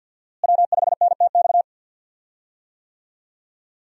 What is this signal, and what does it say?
Morse code, heard as a keyed tone, about 49 wpm, 700 Hz. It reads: W5NTX
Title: USA Call Signs - 5 Characters - Repeated 2x - 50wpm